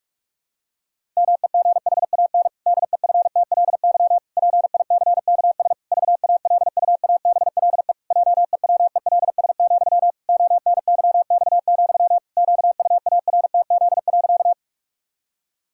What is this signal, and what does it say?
Morse code, heard as a keyed tone, about 46 wpm, 700 Hz. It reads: MEGHAN DEFTLY PICKS VALUABLE JEWELS, ONYX, QUARTZ.
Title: Pangram Challenge v2 - 45wpm